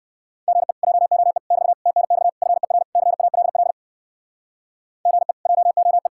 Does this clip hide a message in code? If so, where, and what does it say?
Yes: Morse code from 0.5 to 6.1 s, heard as a keyed tone, about 59 wpm, 700 Hz. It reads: GE JOE <BT> TNX FER CALL GE JOE